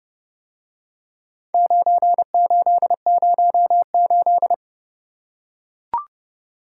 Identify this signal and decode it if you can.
Morse code, heard as a keyed tone, about 27 wpm, 700 Hz. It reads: 9808